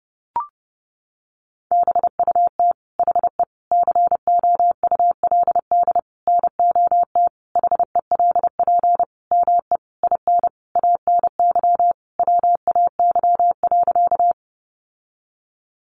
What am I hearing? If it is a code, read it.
Morse code, heard as a keyed tone, about 30 wpm, 700 Hz. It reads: BUT HE COULD NOT HELP ME IN ANY WAY.
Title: Sentences from Top 600 Words 30wpm